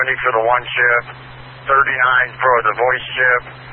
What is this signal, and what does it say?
Emergency Alert System (SAME) tones, heard as harsh buzzing bursts, with speech. It is heard from 0.7 to 3.4 s.